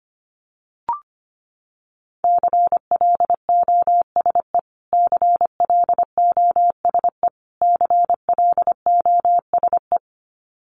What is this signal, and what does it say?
Morse code, heard as a keyed tone, about 25 wpm, 700 Hz. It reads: CLOSE CLOSE CLOSE
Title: Top 500 Words Repeated 25wpm